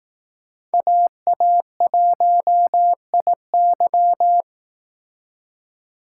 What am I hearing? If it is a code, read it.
Morse code, heard as a keyed tone, about 18 wpm, 700 Hz. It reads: AA1IY
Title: Speed Racing - USA Call Signs 15wpm